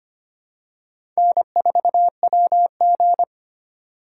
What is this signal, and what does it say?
Morse code, heard as a keyed tone, about 25 wpm, 700 Hz. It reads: N4WG
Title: USA Call Signs - 4 Characters - 3x Character Spacing - 25wpm